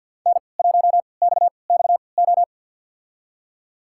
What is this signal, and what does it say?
Morse code, heard as a keyed tone, about 52 wpm, spaced as for 29 wpm, 700 Hz. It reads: N1XXO
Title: USA Call Signs - 5 Characters - 3x Character Spacing - 50wpm